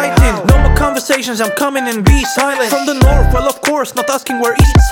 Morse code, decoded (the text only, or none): none